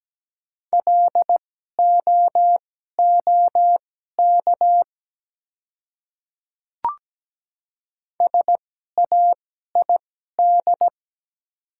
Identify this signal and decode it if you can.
Morse code, heard as a keyed tone, about 17 wpm, spaced as for 12 wpm, 700 Hz. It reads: LOOK SAID
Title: 2x Character Spacing - Top 100 Words - 17wpm